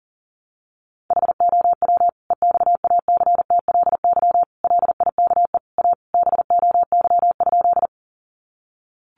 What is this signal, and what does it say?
Morse code, heard as a keyed tone, about 40 wpm, 700 Hz. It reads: HOW EXACTLY LIKE A BOY?
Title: Sentences from Top 900 Words 40wpm